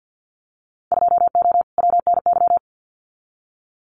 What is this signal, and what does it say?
Morse code, heard as a keyed tone, about 50 wpm, 700 Hz. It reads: 2O WNY
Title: Winter Field Day Exchanges Encore 50wpm